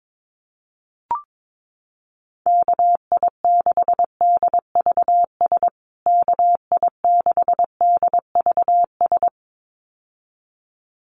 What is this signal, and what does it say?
Morse code, heard as a keyed tone, about 22 wpm, 700 Hz. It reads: KI6DVS KI6DVS
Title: USA Call Signs Repeated 2x 22wpm